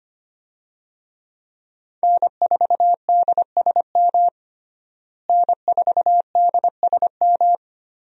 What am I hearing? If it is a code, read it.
Morse code, heard as a keyed tone, about 25 wpm, 700 Hz. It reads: N4DSM N4DSM
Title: USA Call Signs - 5 Characters - Repeated 3x - 25wpm